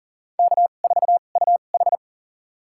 Morse code, heard as a keyed tone, about 40 wpm, spaced as for 29 wpm, 700 Hz. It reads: K4UH
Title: USA Call Signs - 2x Character Spacing - 40wpm